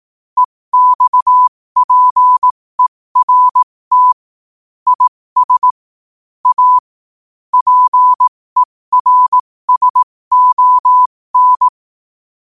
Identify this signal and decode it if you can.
Morse code, heard as a keyed tone, about 18 wpm, spaced as for 15 wpm, 1000 Hz. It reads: EXPERT IS A PERSON
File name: qotdfiles / 15wpm